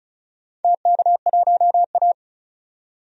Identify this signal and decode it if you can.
Morse code, heard as a keyed tone, about 36 wpm, 700 Hz. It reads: TK1A
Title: DXCC-100 Call Signs - 4 Characters - 35wpm